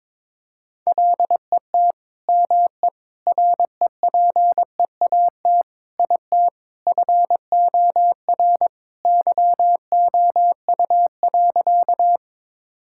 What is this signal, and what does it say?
Morse code, heard as a keyed tone, about 22 wpm, 700 Hz. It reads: LET ME REPEAT IT FOR YOU.
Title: Sentences from Top 1600 Words 22wpm